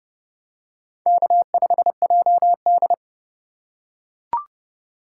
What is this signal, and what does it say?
Morse code, heard as a keyed tone, about 30 wpm, 700 Hz. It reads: K5JD